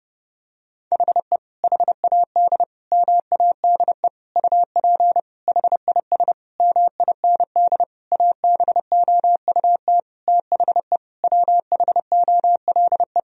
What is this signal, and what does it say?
Morse code, heard as a keyed tone, about 30 wpm, 700 Hz. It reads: HE HAD MADE UP HIS MIND ABOUT THE WHOLE
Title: Sentences from Top 1000 Words - Review Part 3 - 30wpm